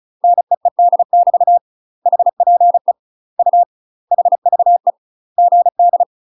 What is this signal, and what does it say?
Morse code, heard as a keyed tone, about 35 wpm, 700 Hz. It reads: NEED <BT> HPE U HVE GD
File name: UK MASTER QSOs TRAINING / 35WPM / UK MASTER QSOs 0014